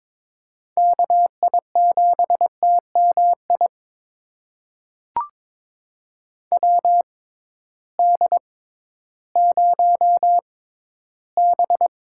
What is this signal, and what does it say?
Morse code, heard as a keyed tone, about 22 wpm, spaced as for 13 wpm, 700 Hz. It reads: KI7TMI WD0B
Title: USA Call Signs - 6 Characters - 6x Character Spacing - 22wpm